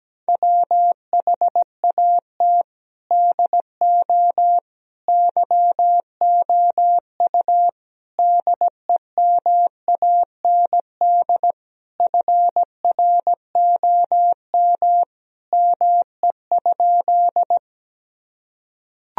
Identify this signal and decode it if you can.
Morse code, heard as a keyed tone, about 17 wpm, 700 Hz. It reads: WHAT DO YOU DEMAND FROM ME?
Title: Sentences from Top 1000 Words - Review Part 4 - 17wpm